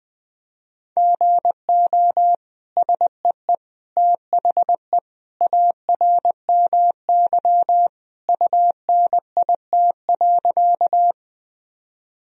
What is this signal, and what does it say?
Morse code, heard as a keyed tone, about 21 wpm, 700 Hz. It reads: GO SEE THE ARMY UNIT.
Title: Sentences from Top 1000 Words - Review Part 2 - 20wpm